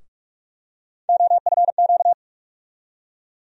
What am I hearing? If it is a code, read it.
Morse code, heard as a keyed tone, about 42 wpm, 700 Hz. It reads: OFQ
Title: USA Call Sign Postfixes - 3 Letters 45wpm